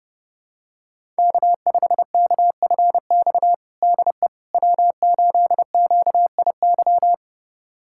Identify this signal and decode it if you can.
Morse code, heard as a keyed tone, about 30 wpm, 700 Hz. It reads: K5KFX DE W8QIY
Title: USA Callsign DE Callsign - 5 Characters - 30wpm